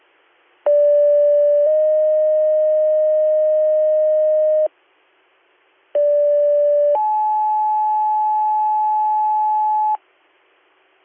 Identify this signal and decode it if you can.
Two-tone page, heard as a one-second tone, then a three-second tone, about 580 then 620 Hz, about 580 then 860 Hz. It is heard from 0.7 to 10.0 s.